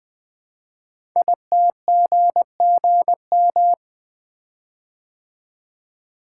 Morse code, heard as a keyed tone, about 20 wpm, 700 Hz. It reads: ITGGM
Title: Code Groups - 5 Letters - 20wpm